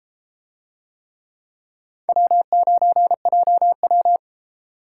Morse code, heard as a keyed tone, about 33 wpm, 700 Hz. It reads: W9JW